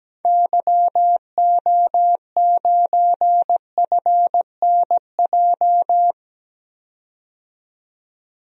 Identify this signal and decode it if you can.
Morse code, heard as a keyed tone, about 17 wpm, 700 Hz. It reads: YO9FNJ
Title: DXCC-100 Call Signs - 6 Characters - 17wpm